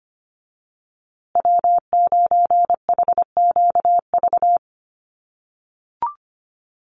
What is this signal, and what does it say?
Morse code, heard as a keyed tone, about 25 wpm, 700 Hz. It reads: W9HQV
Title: USA Call Signs Repeated 2x 25wpm